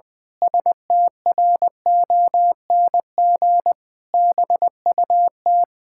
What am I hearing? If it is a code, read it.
Morse code, heard as a keyed tone, about 20 wpm, 700 Hz. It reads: STRONG BUT
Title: Pangram Challenge v2 - 20wpm